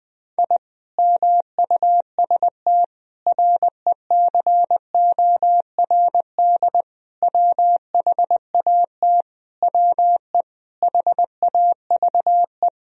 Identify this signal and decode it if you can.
Morse code, heard as a keyed tone, about 21 wpm, 700 Hz. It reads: I MUST RECORD WHAT WE HAVE
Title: Sentences from Top 1000 Words - Review Part 3 - 20wpm